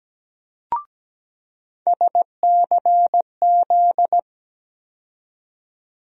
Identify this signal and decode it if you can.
Morse code, heard as a keyed tone, about 17 wpm, 700 Hz. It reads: SCZ
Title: USA Call Sign Postfixes - 3 Letters 17wpm